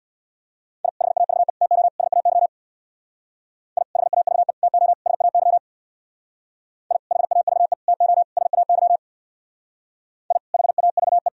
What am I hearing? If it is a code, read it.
Morse code, heard as a keyed tone, about 63 wpm, 700 Hz. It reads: I HAVE TO SAY I HAVE TO SAY I HAVE TO SAY I HAVE